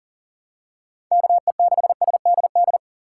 Morse code, heard as a keyed tone, about 40 wpm, 700 Hz. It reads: KE6SDD
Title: USA Call Signs - 6 Characters - 4x Character Spacing - 40wpm